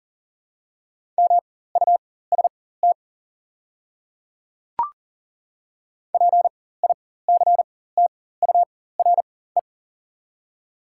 Morse code, heard as a keyed tone, about 40 wpm, spaced as for 19 wpm, 700 Hz. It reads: MUST PICTURE
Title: 4x Character Spacing - Top 200 Words - 40wpm